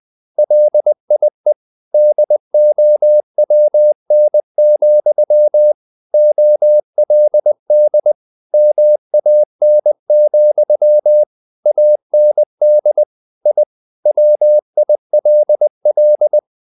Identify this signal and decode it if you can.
Morse code, heard as a keyed tone, about 20 wpm, 600 Hz. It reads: LIE DOWN, OLD MAN, AND I WILL
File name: the old man and the sea part 2 20wpm